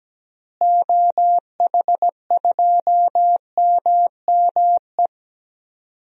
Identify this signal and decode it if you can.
Morse code, heard as a keyed tone, about 17 wpm, 700 Hz. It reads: OH2MME
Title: DXCC-100 Call Signs - 6 Characters - Repeated 3x - 17wpm